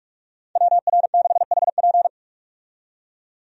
Morse code, heard as a keyed tone, about 47 wpm, 700 Hz. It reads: WR6HP
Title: USA Call Signs - 5 Characters - 45wpm